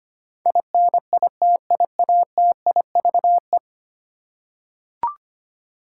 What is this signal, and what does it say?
Morse code, heard as a keyed tone, about 25 wpm, 700 Hz. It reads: INITIATIVE
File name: 10 Letter Words - First 3 Letters Given - 25wpm